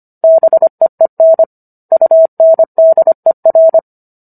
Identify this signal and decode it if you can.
Morse code, heard as a keyed tone, about 25 wpm, 650 Hz. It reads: BEEN UNDER